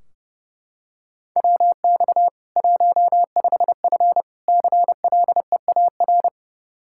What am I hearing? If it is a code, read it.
Morse code, heard as a keyed tone, about 30 wpm, 700 Hz. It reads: WX 15F CLEAR
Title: Practice QSO - Exchange 2 - Info - 30wpm